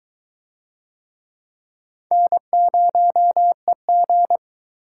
Morse code, heard as a keyed tone, about 22 wpm, 700 Hz. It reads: N0EG